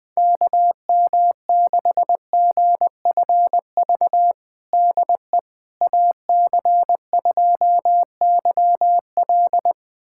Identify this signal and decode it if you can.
Morse code, heard as a keyed tone, about 20 wpm, 700 Hz. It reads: KM6GFV DE AC2YL
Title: USA Callsign DE Callsign 20wpm